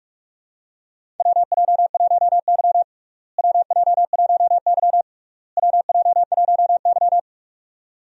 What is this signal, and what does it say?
Morse code, heard as a keyed tone, about 41 wpm, 700 Hz. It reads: WJ1Y WJ1Y WJ1Y
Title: USA Call Signs - 4 Characters - Repeated 3x - 45wpm